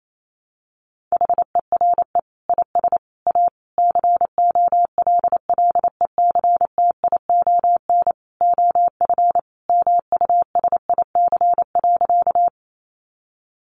Morse code, heard as a keyed tone, about 28 wpm, 700 Hz. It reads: HERE IS A COLLECTION OF MUSIC.
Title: Sentences from Top 1200 Words 28wpm